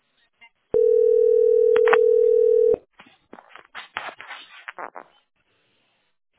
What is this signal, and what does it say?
Ringback tone (North America), 440 + 480 Hz.